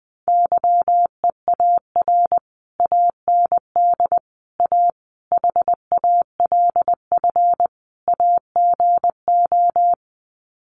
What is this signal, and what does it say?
Morse code, heard as a keyed tone, about 20 wpm, 700 Hz. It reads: YEAR AND A HALF AGO
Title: Sets of 5 Words Encore 20wpm